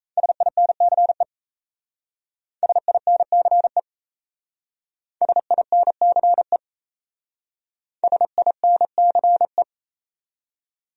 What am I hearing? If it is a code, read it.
Morse code, heard as a keyed tone, about 34 wpm, 700 Hz. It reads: SINCE SINCE SINCE SINCE
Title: Speed-Racing Top 300 Words 28wpm